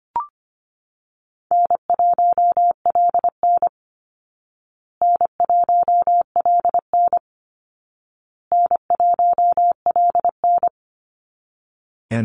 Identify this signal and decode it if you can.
Morse code, heard as a keyed tone, about 25 wpm, 700 Hz. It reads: N1LN N1LN N1LN
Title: Easy Call Signs Repeated 3x - 25wpm